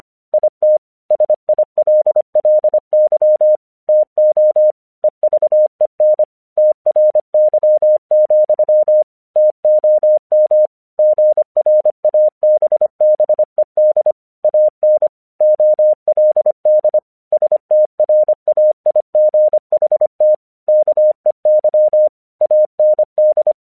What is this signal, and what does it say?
Morse code, heard as a keyed tone, about 25 wpm, 610 Hz. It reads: IT SILLY TO EVEN TRY, TOM GRABBED AN OLD STRAIGHT KEY AND
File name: 1 English 25wpm / AQC30